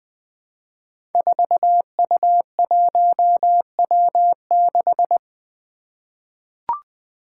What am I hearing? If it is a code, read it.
Morse code, heard as a keyed tone, about 20 wpm, 700 Hz. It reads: 4U1W6
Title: Five Challenging Letter-Number Combinations 20wpm